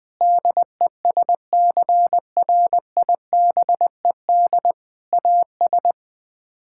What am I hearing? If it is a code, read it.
Morse code, heard as a keyed tone, about 21 wpm, 700 Hz. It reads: DESCRIBED AS
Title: Sets of 6 Words - 20wpm